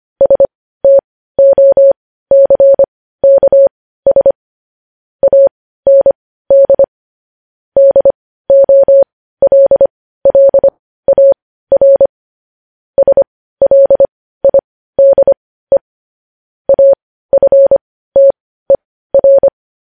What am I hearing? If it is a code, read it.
Morse code, heard as a keyed tone, about 25 wpm, spaced as for 15 wpm, 550 Hz. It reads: STOCKS AND DOLLAR SLIDE AFTER
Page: News headlines in morse code at 15 wpm